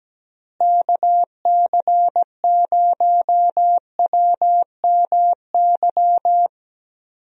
Morse code, heard as a keyed tone, about 17 wpm, 700 Hz. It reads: KC0WMY
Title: USA Call Signs - 6 Characters - Repeated 2x - 17wpm